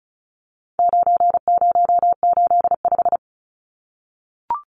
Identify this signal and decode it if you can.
Morse code, heard as a keyed tone, about 34 wpm, 700 Hz. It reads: 9085
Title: Numbers - 4-Digit 35wpm